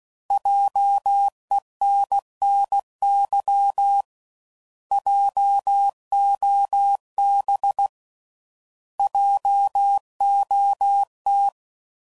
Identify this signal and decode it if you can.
Morse code, heard as a keyed tone, about 16 wpm, 800 Hz. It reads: JENNY JOB JOT